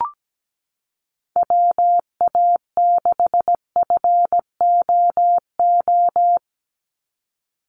Morse code, heard as a keyed tone, about 17 wpm, 700 Hz. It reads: WA6FOO
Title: USA Call Signs 17wpm